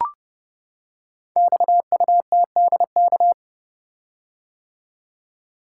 Morse code, heard as a keyed tone, about 30 wpm, 700 Hz. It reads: XUTDK
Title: Code Groups - 5 Letters - 30wpm